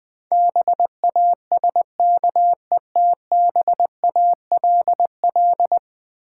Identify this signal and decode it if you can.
Morse code, heard as a keyed tone, about 20 wpm, 700 Hz. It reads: BASKETBALL
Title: Difficult Words with Hints - 20wpm